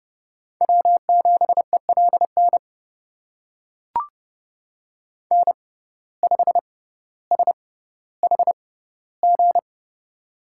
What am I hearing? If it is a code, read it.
Morse code, heard as a keyed tone, about 30 wpm, spaced as for 15 wpm, 700 Hz. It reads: W7ELN N5SHG
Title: USA Call Signs - 5 Characters - 6x Character Spacing - 30wpm